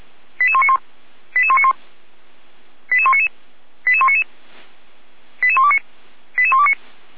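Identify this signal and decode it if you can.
Five-tone selective call (ZVEI), heard as quick runs of five tones, tones of about 70 ms. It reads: 88191 88191 88199 88199 88129 88129
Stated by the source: ZVEI five-tone selective call 88235 received by SDR and logged by Feuerwehr Orlamünde, a fire brigade.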